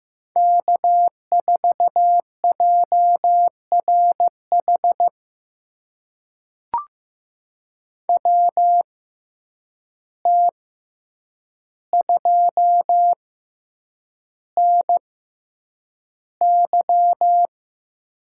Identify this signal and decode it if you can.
Morse code, heard as a keyed tone, about 15 wpm, spaced as for 8 wpm, 700 Hz. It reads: K4JRH WT2NY
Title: USA Call Signs - 5 Characters - 6x Character Spacing - 15wpm